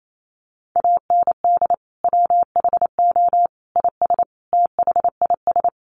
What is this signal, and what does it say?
Morse code, heard as a keyed tone, about 28 wpm, 700 Hz. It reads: AND WHO IS THIS
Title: Sentences from Top 300 Words 28wpm